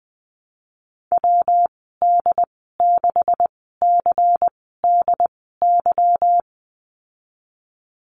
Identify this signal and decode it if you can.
Morse code, heard as a keyed tone, about 20 wpm, spaced as for 15 wpm, 700 Hz. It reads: WD6CDY
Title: 147 Lesson 24g - Letter y - Call Signs 20wpm